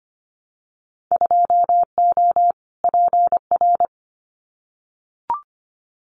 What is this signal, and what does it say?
Morse code, heard as a keyed tone, about 25 wpm, 700 Hz. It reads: 2O PR
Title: Winter Field Day Exchanges Encore 25wpm